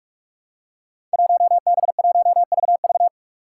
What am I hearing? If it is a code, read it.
Morse code, heard as a keyed tone, about 44 wpm, 700 Hz. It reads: 1B1VV
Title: Five Challenging Letter-Number Combinations 45wpm